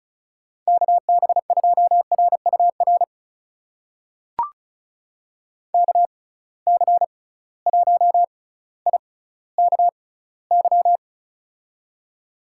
Morse code, heard as a keyed tone, about 34 wpm, spaced as for 18 wpm, 700 Hz. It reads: KB2RUR KC1IKY
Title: USA Call Signs - 6 Characters - 6x Character Spacing - 35wpm